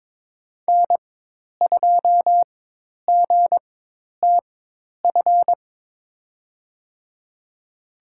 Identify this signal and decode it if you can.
Morse code, heard as a keyed tone, about 22 wpm, spaced as for 10 wpm, 700 Hz. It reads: N2GTF
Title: USA Call Signs - 5 Characters - 4x Character Spacing - 22wpm